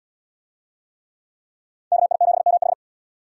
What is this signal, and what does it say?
Morse code, heard as a keyed tone, about 73 wpm, 690 Hz. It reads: KI7MH